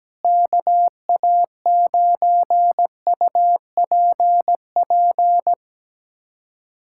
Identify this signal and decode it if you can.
Morse code, heard as a keyed tone, about 17 wpm, 700 Hz. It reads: KA9UPP
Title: USA Callsign DE Callsign 17wpm